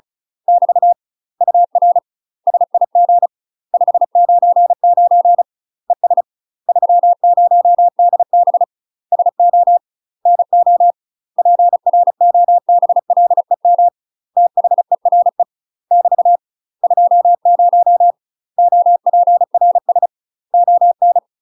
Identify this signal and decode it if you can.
Morse code, heard as a keyed tone, about 34 wpm, 700 Hz. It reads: <BT> UR SIG 599 ES 30DB SO NO PROBLEM THERE <BT> 20 OPRS ON
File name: UK MASTER QSOs TRAINING / 35WPM / UK MASTER QSOs 0015